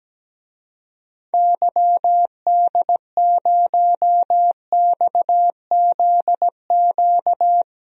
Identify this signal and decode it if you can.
Morse code, heard as a keyed tone, about 17 wpm, 700 Hz. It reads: YD0XZQ
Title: DXCC-100 Call Signs - 6 Characters - 17wpm